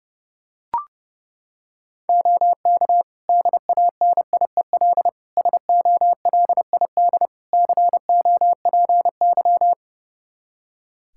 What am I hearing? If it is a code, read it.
Morse code, heard as a keyed tone, about 32 wpm, 700 Hz. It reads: OK DANIEL SOLID COPY
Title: K1USN SST QSO Practice 30wpm